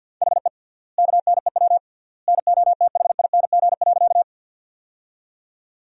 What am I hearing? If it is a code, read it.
Morse code, heard as a keyed tone, about 49 wpm, 700 Hz. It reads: HE KNEW NOTHING.
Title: 3x Word-Spacing - Sentences from Top 500 Words - 50wpm